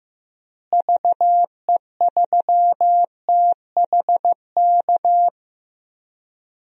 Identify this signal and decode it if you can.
Morse code, heard as a keyed tone, about 15 wpm, 700 Hz. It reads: VE3THK
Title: DXCC-100 Call Signs - 6 Characters - Repeated 3x - 15wpm